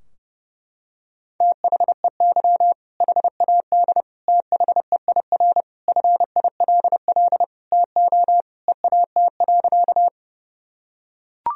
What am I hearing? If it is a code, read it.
Morse code, heard as a keyed tone, about 30 wpm, 700 Hz. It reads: THEY HAD THEIR FILL TO EAT.
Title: Sentences from Top 1000 Words - Review Part 3 - 30wpm